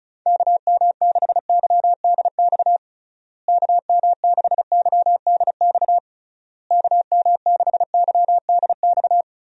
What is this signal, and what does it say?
Morse code, heard as a keyed tone, about 36 wpm, 700 Hz. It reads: KM6YDX KM6YDX KM6YDX
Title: USA Call Signs - 6 Characters - Repeated 3x - 35wpm